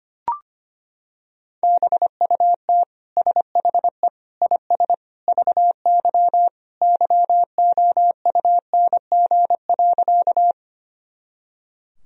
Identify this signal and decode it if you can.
Morse code, heard as a keyed tone, about 25 wpm, 700 Hz. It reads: BUT SHE IS VY YOUNG.
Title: Sentences from Top 400 Words and Abbreviations 25wpm